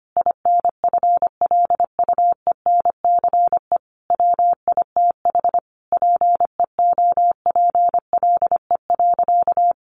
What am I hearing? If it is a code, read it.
Morse code, heard as a keyed tone, about 25 wpm, 700 Hz. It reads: INFLUENCE WITH PEOPLE.
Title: Sentences from Top 1100 Words 25wpm